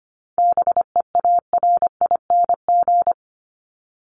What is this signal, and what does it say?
Morse code, heard as a keyed tone, about 25 wpm, 700 Hz. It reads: BEARING
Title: ING Suffix - Top 100 Words with Base Words 25wpm